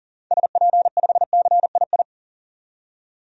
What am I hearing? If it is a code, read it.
Morse code, heard as a keyed tone, about 40 wpm, 700 Hz. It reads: SP5CII
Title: DXCC-100 Call Signs - 6 Characters - 40wpm